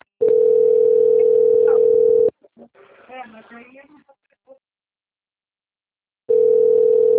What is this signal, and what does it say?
Ringback tone (North America), 440 + 480 Hz, with speech.
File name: out- - - - - .64943